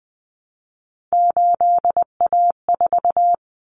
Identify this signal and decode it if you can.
Morse code, heard as a keyed tone, about 19 wpm, 700 Hz. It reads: 8A4
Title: Common Cut Numbers - 3-Digit 20wpm